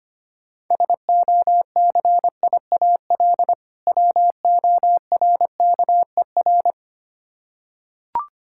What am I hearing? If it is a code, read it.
Morse code, heard as a keyed tone, about 25 wpm, 700 Hz. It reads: SOCIAL WORKER
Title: Practice QSO - Exchange 3 - More Info - 25wpm